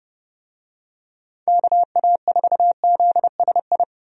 Morse code, heard as a keyed tone, about 30 wpm, 700 Hz. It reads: KA4ZSI
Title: USA Call Signs - 6 Characters - 30wpm